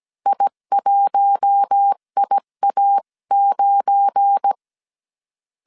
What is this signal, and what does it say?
Morse code, heard as a keyed tone, about 17 wpm, 800 Hz. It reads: I1IA9
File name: 1500C S B 16wpm